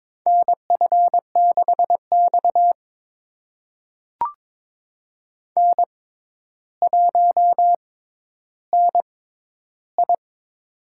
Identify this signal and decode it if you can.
Morse code, heard as a keyed tone, about 22 wpm, spaced as for 11 wpm, 700 Hz. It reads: NF6X N1NI
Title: USA Call Signs - 4 Characters - 6x Character Spacing - 22wpm